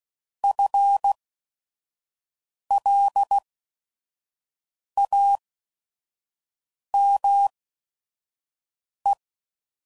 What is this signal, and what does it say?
Morse code, heard as a keyed tone, about 16 wpm, spaced as for 5 wpm, 800 Hz. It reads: FLAME